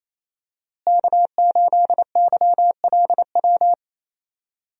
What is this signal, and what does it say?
Morse code, heard as a keyed tone, about 28 wpm, 700 Hz. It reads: K8YLW
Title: USA Call Signs - 5 Characters - Repeated 3x - 28wpm